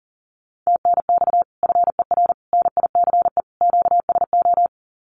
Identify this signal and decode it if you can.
Morse code, heard as a keyed tone, about 40 wpm, 700 Hz. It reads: TNX FER NICE QSO
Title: Common QSO Phrases Encore 40wpm